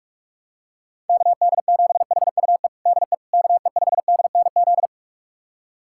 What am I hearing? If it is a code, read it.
Morse code, heard as a keyed tone, about 46 wpm, 700 Hz. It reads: KD7HUE DE KE5DNZ
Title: USA Callsign DE Callsign - 6 Characters - 45wpm